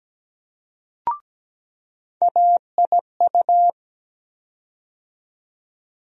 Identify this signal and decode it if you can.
Morse code, heard as a keyed tone, about 17 wpm, 700 Hz. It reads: AIU
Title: USA Call Sign Postfixes - 3 Letters 17wpm